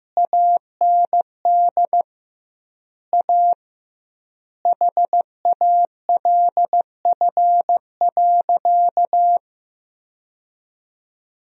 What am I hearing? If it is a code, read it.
Morse code, heard as a keyed tone, about 15 wpm, 700 Hz. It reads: AND A HALF.
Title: 2x Word-Spacing - Sentences from Top 400 Words - 15wpm